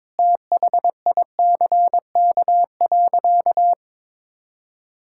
Morse code, heard as a keyed tone, about 22 wpm, 700 Hz. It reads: THICK.
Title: Sentences from Top 1600 Words 22wpm